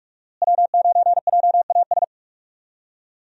Morse code, heard as a keyed tone, about 46 wpm, 700 Hz. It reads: W9JAS
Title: USA Call Signs - 5 Characters - Repeated 3x - 45wpm